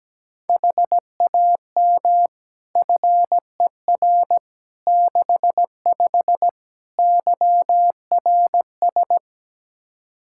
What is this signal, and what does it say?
Morse code, heard as a keyed tone, about 17 wpm, 700 Hz. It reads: HAM FER 65 YRS
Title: K1USN SST QSO Practice 17wpm